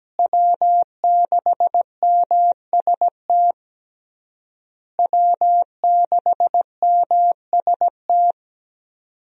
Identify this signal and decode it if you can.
Morse code, heard as a keyed tone, about 17 wpm, 700 Hz. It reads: W6MST W6MST
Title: USA Call Signs - 5 Characters - Repeated 3x - 17wpm